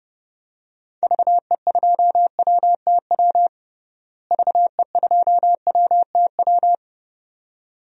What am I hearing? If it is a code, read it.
Morse code, heard as a keyed tone, about 30 wpm, 700 Hz. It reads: VE2WTW VE2WTW
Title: DXCC-100 Call Signs - 6 Characters - Repeated 2x - 30wpm